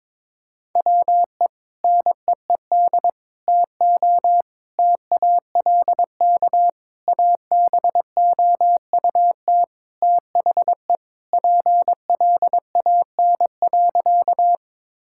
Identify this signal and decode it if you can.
Morse code, heard as a keyed tone, about 22 wpm, 700 Hz. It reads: WE NEED TO TALK ABOUT THE PLAN.
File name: practice / 22wpm / Sentences with 4-Word Phrases - 22wpm